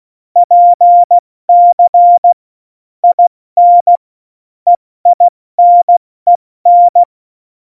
Morse code, heard as a keyed tone, about 16 wpm, 700 Hz. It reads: PC IN EINEN
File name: Morsekurs / Witze / Witz 08. Januar 2019, 70 F80